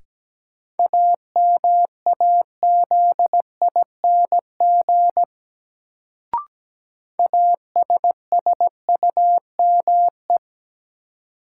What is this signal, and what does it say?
Morse code, heard as a keyed tone, about 17 wpm, 700 Hz. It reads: AMAZING ASSUME
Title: Sentences from Top 1500 Words - New Word Drill - 17wpm